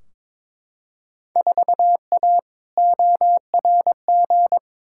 Morse code, heard as a keyed tone, about 22 wpm, 700 Hz. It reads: 4A ORG
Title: Field Day Exchanges - 22wpm